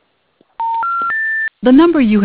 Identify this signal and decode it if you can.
Special information tone, with speech.
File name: recordings / RE3e0d81a061f2e70eb9c15ebee47f9fbd